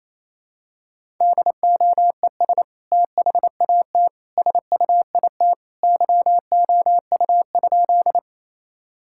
Morse code, heard as a keyed tone, about 28 wpm, 700 Hz. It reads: DOES THAT SUIT YOU?
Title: Sentences from Top 1600 Words 28wpm